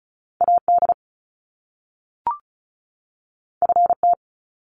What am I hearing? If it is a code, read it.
Morse code, heard as a keyed tone, about 36 wpm, 700 Hz. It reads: AD FT
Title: Top Two Letter Combinations in Words 35wpm